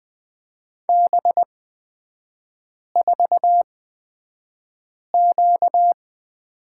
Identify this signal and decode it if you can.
Morse code, heard as a keyed tone, about 20 wpm, spaced as for 5 wpm, 690 Hz. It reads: B4Q
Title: 255 Lesson 41e - Period - Three Characters 20wpm